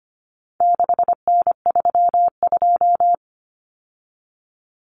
Morse code, heard as a keyed tone, about 25 wpm, 700 Hz. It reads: 6N32
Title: Common Cut Numbers - 4-Digit 25wpm